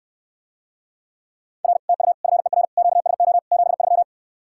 Morse code, heard as a keyed tone, about 69 wpm, 700 Hz. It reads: R TU FER QSO 73